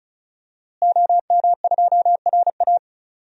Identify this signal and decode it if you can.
Morse code, heard as a keyed tone, about 36 wpm, 700 Hz. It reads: OM2RA
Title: DXCC-100 Call Signs - 5 Characters - Repeated 2x - 35wpm